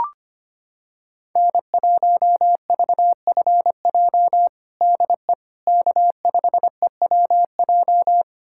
Morse code, heard as a keyed tone, about 25 wpm, 700 Hz. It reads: N1VFJ DE K5EWJ
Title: USA Callsign DE Callsign - 5 Characters - 25wpm